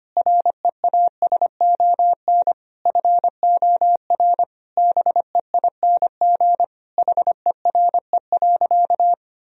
Morse code, heard as a keyed tone, about 25 wpm, 700 Hz. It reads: REASON FOR BEING HERE.
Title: Sentences from Top 1000 Words - Review Part 2 - 25wpm